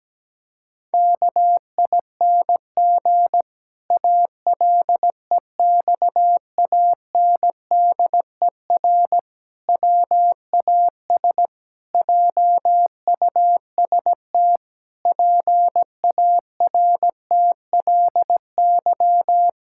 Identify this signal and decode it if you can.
Morse code, heard as a keyed tone, about 17 wpm, 700 Hz. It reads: KING ALEXANDER WAS JUST PARTLY
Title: Pangram Challenge v2 - 17wpm